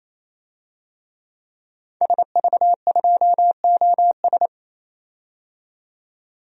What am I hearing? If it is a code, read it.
Morse code, heard as a keyed tone, about 28 wpm, 700 Hz. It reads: SV2OS
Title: DXCC-100 Call Signs - 5 Characters - 28wpm